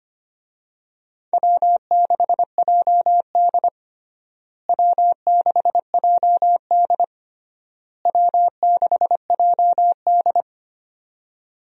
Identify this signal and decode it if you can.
Morse code, heard as a keyed tone, about 25 wpm, 700 Hz. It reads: W6JD W6JD W6JD
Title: USA Call Signs - 4 Characters - Repeated 3x - 25wpm